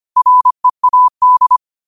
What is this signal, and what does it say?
Morse code, heard as a keyed tone, about 25 wpm, 1000 Hz. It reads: READ